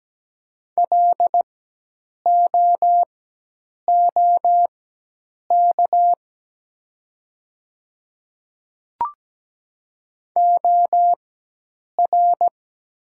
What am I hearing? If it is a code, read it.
Morse code, heard as a keyed tone, about 17 wpm, spaced as for 8 wpm, 700 Hz. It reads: LOOK OR
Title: 4x Character Spacing - Top 100 Words - 17wpm